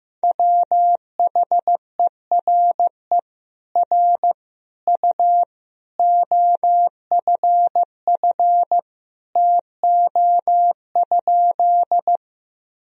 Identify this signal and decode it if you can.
Morse code, heard as a keyed tone, about 15 wpm, 700 Hz. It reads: WHERE R U OFF TO?
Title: Sentences from Top 200 Words and Abbreviations 15wpm